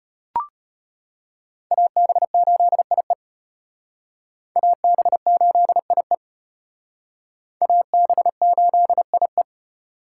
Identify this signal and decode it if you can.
Morse code, heard as a keyed tone, about 32 wpm, 700 Hz. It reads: AB8IE AB8IE AB8IE